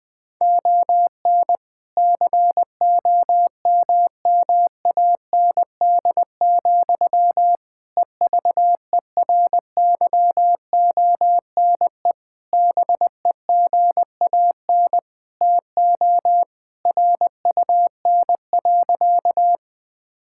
Morse code, heard as a keyed tone, about 21 wpm, 700 Hz. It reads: ON COMMAND, EVERYONE BEGAN TO RUN.
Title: Sentences from Top 1600 Words 20wpm